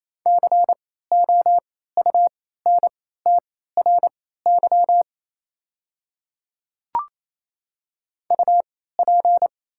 Morse code, heard as a keyed tone, about 27 wpm, spaced as for 16 wpm, 700 Hz. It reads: COUNTRY UP
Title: 3x Character Spacing - Top 500 Words - 28wpm